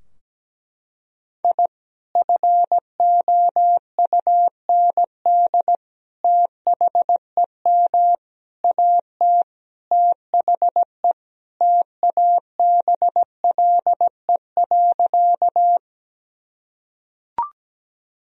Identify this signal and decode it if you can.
Morse code, heard as a keyed tone, about 17 wpm, 700 Hz. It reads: I FOUND THEM AT THE TABLE.
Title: Sentences from Top 1000 Words - Review Part 2 - 17wpm